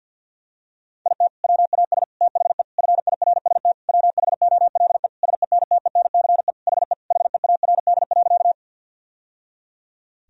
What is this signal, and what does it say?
Morse code, heard as a keyed tone, about 53 wpm, 700 Hz. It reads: IT WAS THE FIRST WHOLE SENTENCE HE HEARD.
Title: Sentences from Top 1000 Words - Review Part 1 - 50wpm